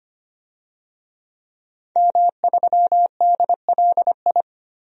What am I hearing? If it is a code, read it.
Morse code, heard as a keyed tone, about 25 wpm, 700 Hz. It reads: M3DLI
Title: DXCC-100 Call Signs - 5 Characters - 25wpm